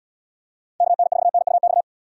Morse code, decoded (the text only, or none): KN4NUX